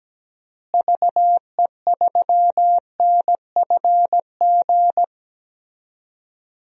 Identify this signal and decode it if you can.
Morse code, heard as a keyed tone, about 17 wpm, 700 Hz. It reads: VE3NFG
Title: DXCC-100 Call Signs - 6 Characters - Repeated 2x - 17wpm